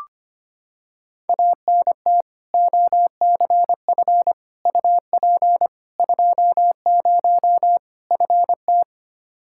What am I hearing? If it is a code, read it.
Morse code, heard as a keyed tone, about 25 wpm, 700 Hz. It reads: ANT OCF UP 20 FT
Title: Practice QSO - Exchange 2 - Info - 25wpm